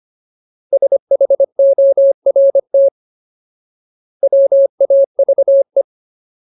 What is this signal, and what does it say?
Morse code, heard as a keyed tone, about 25 wpm, 550 Hz. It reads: SHORT WAVE
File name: Ultimate Word List / 25WPM Ultimate